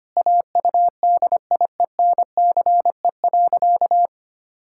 Morse code, heard as a keyed tone, about 25 wpm, 700 Hz. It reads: AUDIENCE.
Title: Sentences from Top 1000 Words - Review Part 5 - 25wpm